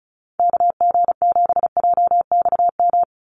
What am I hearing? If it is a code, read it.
Morse code, heard as a keyed tone, about 34 wpm, 700 Hz. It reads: KG7JXM
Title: USA Call Signs 35wpm